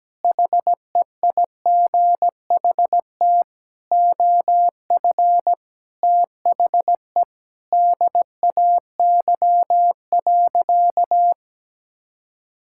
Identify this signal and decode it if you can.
Morse code, heard as a keyed tone, about 17 wpm, 700 Hz. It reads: HEIGHT OF THE DAY.
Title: Sentences from Top 1600 Words 17wpm